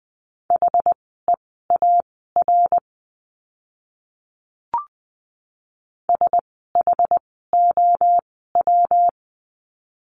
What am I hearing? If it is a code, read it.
Morse code, heard as a keyed tone, about 20 wpm, spaced as for 15 wpm, 700 Hz. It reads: HEAR SHOW
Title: 206 Lesson 33f - Letter j - Words 20wpm